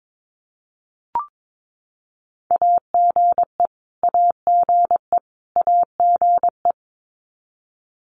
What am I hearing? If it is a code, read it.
Morse code, heard as a keyed tone, about 22 wpm, 700 Hz. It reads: AGE AGE AGE
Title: Top 500 Words Repeated 22wpm